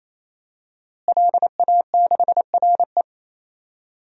Morse code, heard as a keyed tone, about 28 wpm, 700 Hz. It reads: LA6RE
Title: DXCC-100 Call Signs - 5 Characters - Repeated 3x - 28wpm